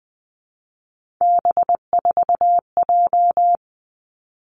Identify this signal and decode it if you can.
Morse code, heard as a keyed tone, about 19 wpm, 700 Hz. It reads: B4J
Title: Three Challenging Letter-Number Combinations 20wpm